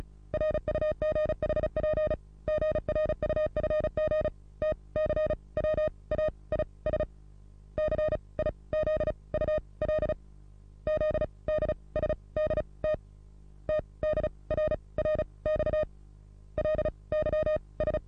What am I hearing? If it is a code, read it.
Morse code, heard as a keyed tone, about 35 wpm, spaced as for 22 wpm, 620 Hz. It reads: RUGHP GRUFG TCWAI SCIZU LZDSD TTDRR XLYS
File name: repos / Morsekurs / DJ4UF Morsen27